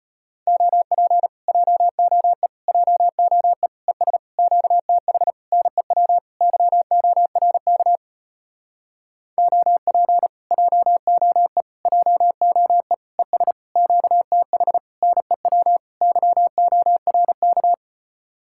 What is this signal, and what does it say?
Morse code, heard as a keyed tone, about 33 wpm, 700 Hz. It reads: OP JOE JOE ES QTH NEW YORK OP JOE JOE ES QTH NEW YORK